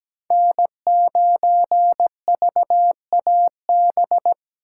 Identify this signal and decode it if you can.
Morse code, heard as a keyed tone, about 17 wpm, 700 Hz. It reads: N9VAB
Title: USA Call Signs - 5 Characters - 8x Character Spacing - 17wpm